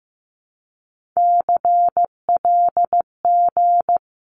Morse code, heard as a keyed tone, about 15 wpm, 700 Hz. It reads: CLG
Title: Abbreviations 15wpm